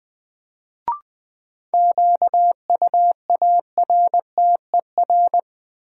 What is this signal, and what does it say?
Morse code, heard as a keyed tone, about 20 wpm, 700 Hz. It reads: QUARTER
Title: Sentences from Top 1300 Words - New Word Drill Alt - 20wpm